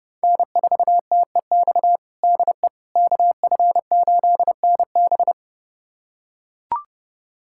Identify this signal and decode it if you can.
Morse code, heard as a keyed tone, about 30 wpm, 700 Hz. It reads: N4TEX DE KF8NB
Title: USA Callsign DE Callsign - 5 Characters - 30wpm